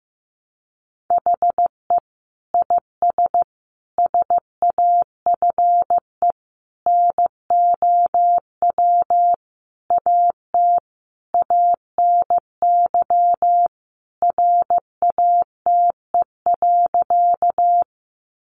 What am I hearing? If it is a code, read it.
Morse code, heard as a keyed tone, about 15 wpm, 700 Hz. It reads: HE IS SAFE NOW AT ANY RATE.
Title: Sentences from Top 900 Words 15wpm